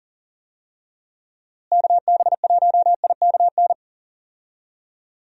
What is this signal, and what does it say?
Morse code, heard as a keyed tone, about 40 wpm, 700 Hz. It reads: KB1IKN
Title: USA Call Signs - 6 Characters - Repeated 2x - 40wpm